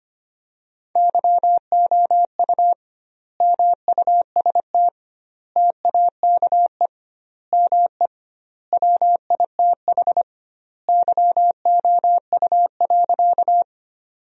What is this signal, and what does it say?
Morse code, heard as a keyed tone, about 25 wpm, 700 Hz. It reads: YOU MUST TAKE ME WITH YOU.
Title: 2x Word-Spacing - Sentences from Top 200 Words - 25wpm